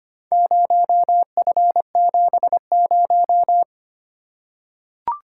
Morse code, heard as a keyed tone, about 25 wpm, 700 Hz. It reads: 0F70
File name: practice / 25wpm / Four Letter-Number Combinations - 25wpm